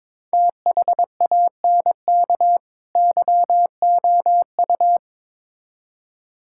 Morse code, heard as a keyed tone, about 22 wpm, 700 Hz. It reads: THANK YOU
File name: Build A Phrase - 2-Word Phrases - 22wpm